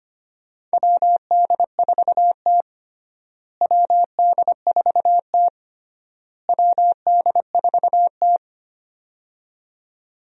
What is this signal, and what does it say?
Morse code, heard as a keyed tone, about 25 wpm, 700 Hz. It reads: WD4T WD4T WD4T
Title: USA Call Signs - 4 Characters - Repeated 3x - 25wpm